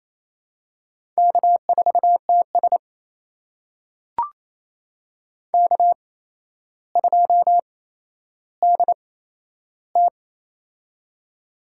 Morse code, heard as a keyed tone, about 28 wpm, spaced as for 12 wpm, 700 Hz. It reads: K4TS K2DT
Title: USA Call Signs - 4 Characters - 8x Character Spacing - 28wpm